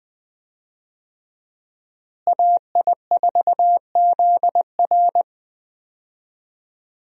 Morse code, heard as a keyed tone, about 21 wpm, 700 Hz. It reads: AI4ZR
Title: USA Call Signs - 5 Characters - Repeated 3x - 20wpm